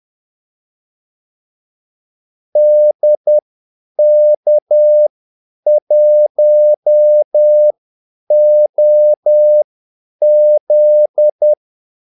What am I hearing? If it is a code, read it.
Morse code, heard as a keyed tone, about 10 wpm, spaced as for 8 wpm, 600 Hz. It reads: DK1OZ
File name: Callsigns-10-8wpm